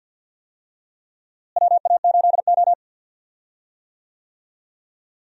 Morse code, heard as a keyed tone, about 47 wpm, 700 Hz. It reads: WA8O
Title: USA Call Signs - 4 Characters - Repeated 3x - 50wpm